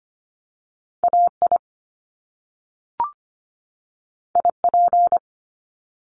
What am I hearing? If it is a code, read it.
Morse code, heard as a keyed tone, about 25 wpm, 700 Hz. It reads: AI IP
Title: Top Two Letter Combinations in Words 25wpm